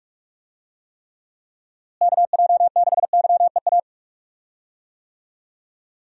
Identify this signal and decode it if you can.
Morse code, heard as a keyed tone, about 46 wpm, 700 Hz. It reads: KJ6YEA
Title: USA Call Signs - 6 Characters - Repeated 2x - 45wpm